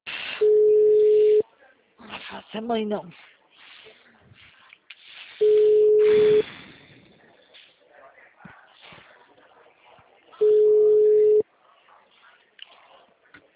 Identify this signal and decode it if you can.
Ringback tone (Europe, 425 Hz), with speech.